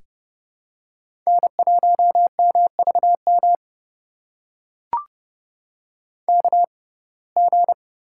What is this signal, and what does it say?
Morse code, heard as a keyed tone, about 31 wpm, spaced as for 22 wpm, 700 Hz. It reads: N1MVM KG
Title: USA Call Signs - 6x Character Spacing - 30wpm